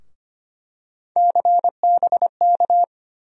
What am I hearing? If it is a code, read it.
Morse code, heard as a keyed tone, about 25 wpm, 700 Hz. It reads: CBK
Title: USA Call Sign Postfixes - 3 Letters 25wpm